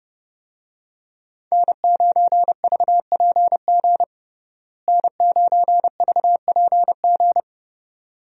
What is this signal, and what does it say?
Morse code, heard as a keyed tone, about 31 wpm, 700 Hz. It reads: N9VPG N9VPG
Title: USA Call Signs - 5 Characters - Repeated 2x - 30wpm